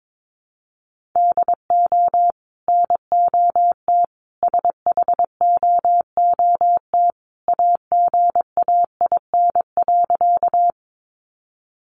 Morse code, heard as a keyed tone, about 22 wpm, 700 Hz. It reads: DO NOT SHOOT AGAIN.
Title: Sentences from Top 1200 Words 22wpm